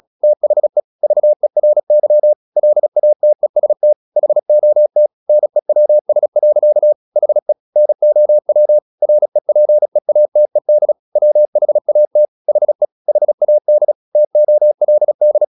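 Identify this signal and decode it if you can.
Morse code, heard as a keyed tone, about 36 wpm, 600 Hz. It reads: THE VERY LATEST HOT NEWS. HE NOW REPEATED WHAT HE HAD TOLD
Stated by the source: Morse practice at 36 wpm, text of Ken Follett's novel Never, from CW med Gnister.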